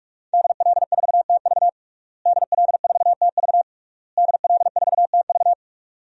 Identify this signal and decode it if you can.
Morse code, heard as a keyed tone, about 46 wpm, 700 Hz. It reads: DL4TV DL4TV DL4TV
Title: DXCC-100 Call Signs - 5 Characters - Repeated 3x - 45wpm